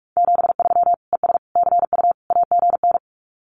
Morse code, heard as a keyed tone, about 44 wpm, 700 Hz. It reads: 73 ES CU AGN
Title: Common QSO Phrases Encore 45wpm